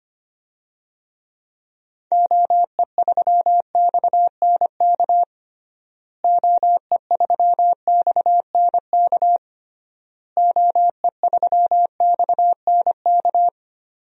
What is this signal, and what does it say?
Morse code, heard as a keyed tone, about 25 wpm, 700 Hz. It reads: OE3XNK OE3XNK OE3XNK
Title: DXCC-100 Call Signs - 6 Characters - Repeated 3x - 25wpm